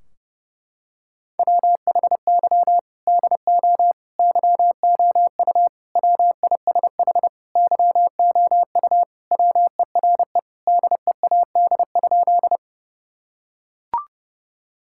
Morse code, heard as a keyed tone, about 30 wpm, 700 Hz. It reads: WHY DO YOU WISH YOU WERE DEAD?
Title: Sentences from Top 1600 Words 30wpm